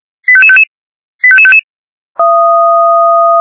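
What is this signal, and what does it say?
Five-tone selective call (ZVEI), heard as quick runs of five tones, tones of about 70 ms. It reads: 85555 85555
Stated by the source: ZVEI five-tone selective call received on SDR.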